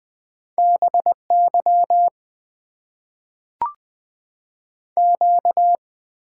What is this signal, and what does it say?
Morse code, heard as a keyed tone, about 20 wpm, 700 Hz. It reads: BY Q
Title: Top 90 QSO Elements - 6x Character Spacing - 20wpm